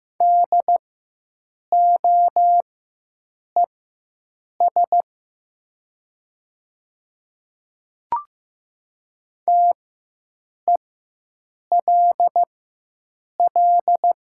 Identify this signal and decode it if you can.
Morse code, heard as a keyed tone, about 15 wpm, spaced as for 7 wpm, 700 Hz. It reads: DOES TELL
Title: 4x Character Spacing - Top 200 Words - 15wpm